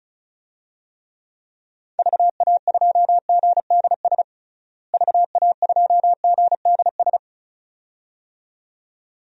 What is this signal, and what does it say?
Morse code, heard as a keyed tone, about 36 wpm, 700 Hz. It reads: VA2GDS VA2GDS
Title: DXCC-100 Call Signs - 6 Characters - Repeated 2x - 35wpm